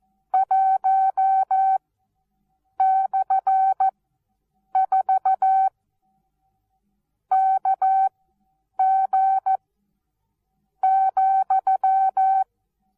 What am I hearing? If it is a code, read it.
Morse code, heard as a keyed tone, about 14 wpm, spaced as for 6 wpm, 770 Hz. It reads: 1/4KG,
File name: Onlinelehrgang / morsen / DJ4UF Morsen19